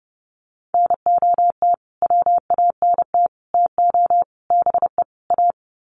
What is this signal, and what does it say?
Morse code, heard as a keyed tone, about 28 wpm, 700 Hz. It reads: NOT WANT TO BE A
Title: Sets of 5 Words Encore 30wpm